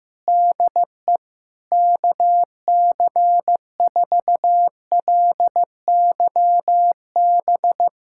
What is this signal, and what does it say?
Morse code, heard as a keyed tone, about 15 wpm, 700 Hz. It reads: DE KC4LYB
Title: USA Callsign DE Callsign - 6 Characters - 15wpm